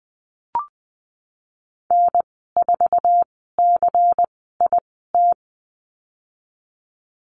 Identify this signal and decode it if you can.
Morse code, heard as a keyed tone, about 19 wpm, spaced as for 14 wpm, 700 Hz. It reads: N4CIT
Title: 122 Lesson 20g - Number 3 - Call Signs 20wpm